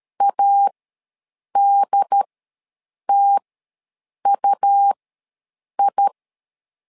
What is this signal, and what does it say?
Morse code, heard as a keyed tone, about 13 wpm, spaced as for 7 wpm, 800 Hz. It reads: ADTUI
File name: LEZIONI / 2 UHDN / 2 UHD ws 12wpm 1722